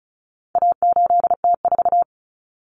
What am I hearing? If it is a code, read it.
Morse code, heard as a keyed tone, about 34 wpm, 700 Hz. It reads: A8T4
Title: Common Cut Numbers - 4-Digit 35wpm